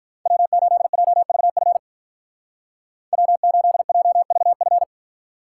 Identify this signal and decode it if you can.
Morse code, heard as a keyed tone, about 45 wpm, 700 Hz. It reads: W8JVF W8JVF